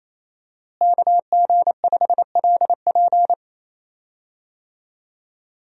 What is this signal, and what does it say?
Morse code, heard as a keyed tone, about 28 wpm, 700 Hz. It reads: KG5LP
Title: USA Call Signs - 5 Characters - Repeated 3x - 28wpm